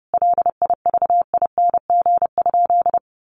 Morse code, heard as a keyed tone, about 30 wpm, 700 Hz. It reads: LIVING?
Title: Sentences from Top 1100 Words 30wpm